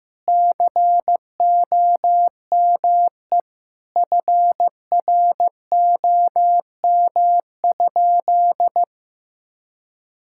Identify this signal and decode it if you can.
Morse code, heard as a keyed tone, about 16 wpm, 700 Hz. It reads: COME FROM?
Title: Sentences from Top 1000 Words - Review Part 6 - 15wpm